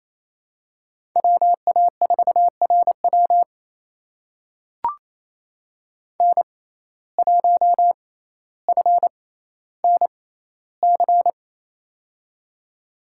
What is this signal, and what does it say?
Morse code, heard as a keyed tone, about 28 wpm, spaced as for 14 wpm, 700 Hz. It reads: WA4RW N1FNC
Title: USA Call Signs - 5 Characters - 6x Character Spacing - 28wpm